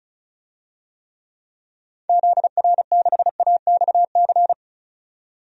Morse code, heard as a keyed tone, about 34 wpm, 700 Hz. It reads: ZR6AXC
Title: DXCC-100 Call Signs - 6 Characters - 35wpm